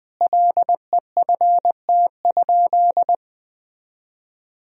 Morse code, heard as a keyed tone, about 20 wpm, 700 Hz. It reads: LEFT?
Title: Sentences from Top 300 Words and Abbreviations 20wpm